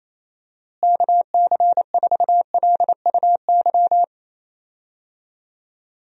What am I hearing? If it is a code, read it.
Morse code, heard as a keyed tone, about 27 wpm, 700 Hz. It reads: KC4LUY
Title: USA Call Signs - 6 Characters - Repeated 3x - 28wpm